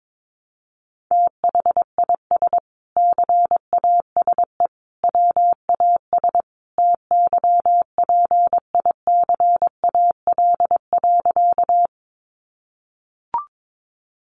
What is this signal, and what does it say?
Morse code, heard as a keyed tone, about 22 wpm, 700 Hz. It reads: THIS CASE WAS TYPICAL.
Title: Sentences from Top 1100 Words 22wpm